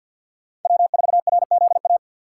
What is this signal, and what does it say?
Morse code, heard as a keyed tone, about 52 wpm, 700 Hz. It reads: W4RZA
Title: USA Call Signs - 5 Characters - 2x Character Spacing - 50wpm